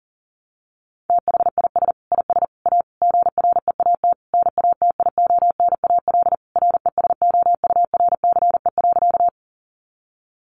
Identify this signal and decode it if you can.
Morse code, heard as a keyed tone, about 37 wpm, 700 Hz. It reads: THIS IS A GREAT NATIONAL RESOURCE.
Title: Sentences from Top 1300 Words 40wpm